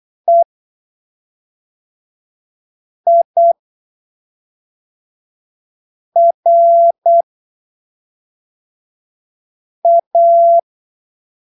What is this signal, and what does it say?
Morse code, heard as a keyed tone, about 8 wpm, spaced as for 3 wpm, 680 Hz. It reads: EIRA
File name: arquivos / texto / texto 14 08ppm